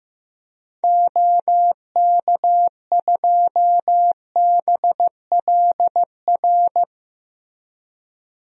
Morse code, heard as a keyed tone, about 15 wpm, 700 Hz. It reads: OK2BLR
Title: DXCC-100 Call Signs - 6 Characters - Repeated 3x - 15wpm